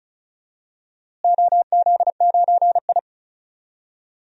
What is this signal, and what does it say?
Morse code, heard as a keyed tone, about 36 wpm, 700 Hz. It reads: OZ9I
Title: DXCC-100 Call Signs - 4 Characters - Repeated 2x - 35wpm